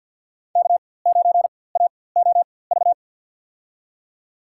Morse code, heard as a keyed tone, about 50 wpm, spaced as for 23 wpm, 700 Hz. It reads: K9AOV
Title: USA Call Signs - 4x Character Spacing - 50wpm